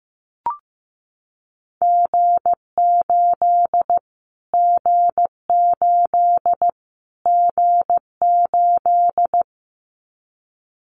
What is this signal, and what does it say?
Morse code, heard as a keyed tone, about 15 wpm, 700 Hz. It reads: G8 G8 G8
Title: Callsign Prefixes for Top 25 DXCC - Repeated 3x - 15wpm